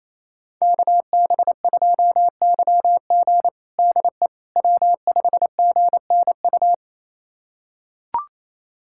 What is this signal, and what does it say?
Morse code, heard as a keyed tone, about 27 wpm, 700 Hz. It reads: KB2YG DE W5GNU
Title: USA Callsign DE Callsign - 5 Characters - 28wpm